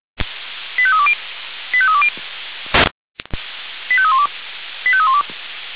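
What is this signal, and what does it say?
Five-tone selective call (ZVEI), heard as quick runs of five tones, tones of about 70 ms. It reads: 85320 85320 85312 85312
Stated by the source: ZVEI five-tone selective call received on SDR.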